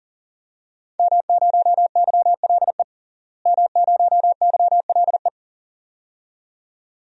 Morse code, heard as a keyed tone, about 40 wpm, 700 Hz. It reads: M0YLE M0YLE
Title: DXCC-100 Call Signs - 5 Characters - Repeated 2x - 40wpm